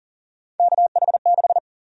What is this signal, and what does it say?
Morse code, heard as a keyed tone, about 40 wpm, 700 Hz. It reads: KH6
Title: USA Call Sign Prefixes - Repeated 2x - 40wpm